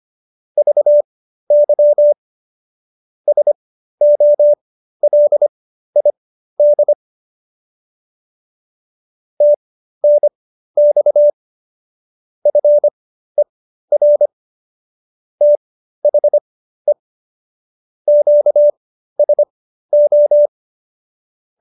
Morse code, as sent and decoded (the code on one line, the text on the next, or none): ...- -.-- / ... --- .-.. .. -.. / - -. -..- / ..-. .-. / - .... / --.- ... ---
VY SOLID TNX FER THE QSO